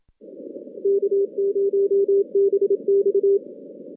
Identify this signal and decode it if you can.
Morse code, heard as a keyed tone, about 26 wpm, 400 Hz. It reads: K0BX
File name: 2016 JAN CW NAQP / 2016 Jan naqp k5ot - 0-10